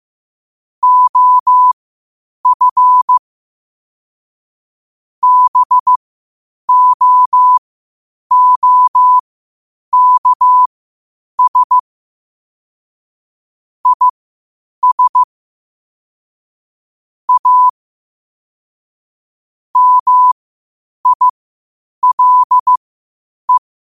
Morse code, heard as a keyed tone, about 15 wpm, spaced as for 8 wpm, 1000 Hz. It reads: OF BOOKS IS A MILE
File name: qotdfiles / 8wpm